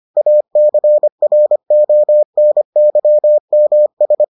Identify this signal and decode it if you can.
Morse code, heard as a keyed tone, about 25 wpm, 600 Hz. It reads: ACRONYMS